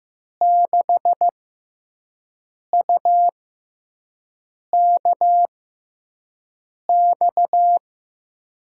Morse code, heard as a keyed tone, about 15 wpm, spaced as for 5 wpm, 700 Hz. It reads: 6UKX
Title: USA Call Signs - 5 Characters - 6x Character Spacing - 15wpm